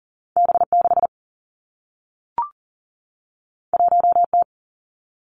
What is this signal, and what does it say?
Morse code, heard as a keyed tone, about 40 wpm, 700 Hz. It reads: B6 1T
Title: Two Challenging Letter-Number Combinations 40wpm